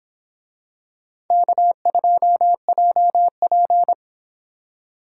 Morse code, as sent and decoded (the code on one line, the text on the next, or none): -.- ..--- .--- .--.
K2JP